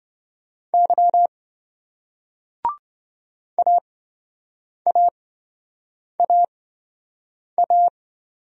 Morse code, heard as a keyed tone, about 28 wpm, spaced as for 7 wpm, 700 Hz. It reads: Y AAAA